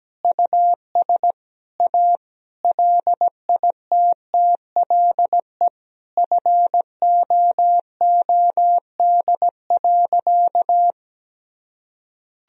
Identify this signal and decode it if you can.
Morse code, heard as a keyed tone, about 17 wpm, 700 Hz. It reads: US A LITTLE FOOD.
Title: Sentences from Top 1000 Words - Review Part 4 - 17wpm